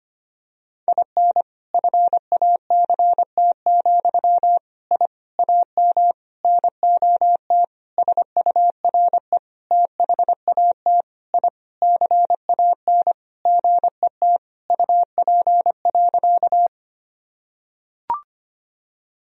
Morse code, heard as a keyed tone, about 25 wpm, 700 Hz. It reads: IN FACT, I AM NOT SURE THAT I CAN GET UP.
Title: Sentences from Top 1000 Words - Review Part 3 - 25wpm